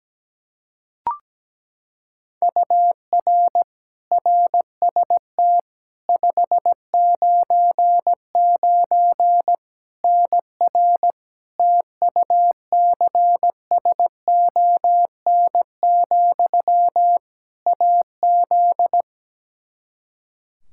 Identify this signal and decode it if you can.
Morse code, heard as a keyed tone, about 17 wpm, 700 Hz. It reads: UR RST 599 NR TUCSON, AZ
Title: Practice QSO - Exchange 1 - Beginning - 17wpm